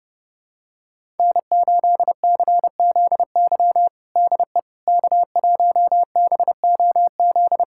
Morse code, heard as a keyed tone, about 30 wpm, 700 Hz. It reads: N8CZY DE K1BOZ
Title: USA Callsign DE Callsign - 5 Characters - 30wpm